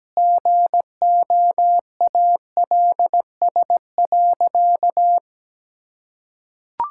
Morse code, heard as a keyed tone, about 17 wpm, 700 Hz. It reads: GOALS.
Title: Sentences from Top 1000 Words - Review Part 6 - 17wpm